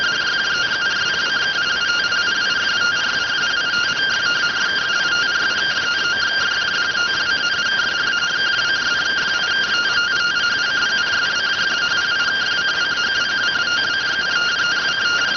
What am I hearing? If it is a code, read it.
Two-tone teleprinter signal, about 170 Hz shift.